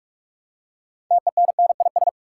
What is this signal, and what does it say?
Morse code, heard as a keyed tone, about 46 wpm, 700 Hz. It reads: TENNIS
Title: Six Letter Words - 45wpm